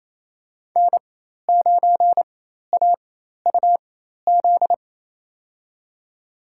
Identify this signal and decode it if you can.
Morse code, heard as a keyed tone, about 27 wpm, spaced as for 13 wpm, 700 Hz. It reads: N9AUZ
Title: USA Call Signs - 5 Characters - 4x Character Spacing - 28wpm